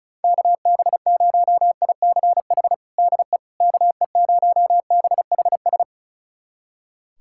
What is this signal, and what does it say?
Morse code, heard as a keyed tone, about 36 wpm, 700 Hz. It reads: KB0ICH DE KE0BHS
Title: USA Callsign DE Callsign - 6 Characters - 35wpm